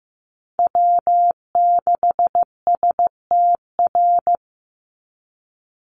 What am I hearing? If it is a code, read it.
Morse code, heard as a keyed tone, about 15 wpm, 700 Hz. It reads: W6STR
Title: NA SOTA Chaser Callsigns 15wpm